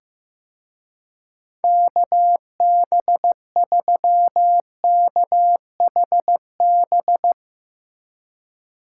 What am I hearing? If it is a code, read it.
Morse code, heard as a keyed tone, about 15 wpm, 700 Hz. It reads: KB3KHB
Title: Speed Racing - USA Call Signs 15wpm